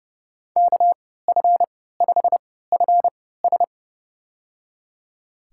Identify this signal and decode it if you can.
Morse code, heard as a keyed tone, about 32 wpm, spaced as for 17 wpm, 700 Hz. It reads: KF5FS
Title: USA Call Signs - 5 Characters - 3x Character Spacing - 30wpm